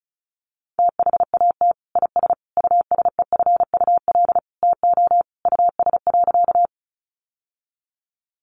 Morse code, heard as a keyed tone, about 34 wpm, 700 Hz. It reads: THAT IS USEFUL TO US.
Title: Sentences from Top 700 Words 35wpm